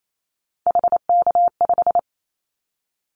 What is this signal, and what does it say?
Morse code, heard as a keyed tone, about 28 wpm, 700 Hz. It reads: HK5
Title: Callsign Prefixes for Top 100 DXCC - 28wpm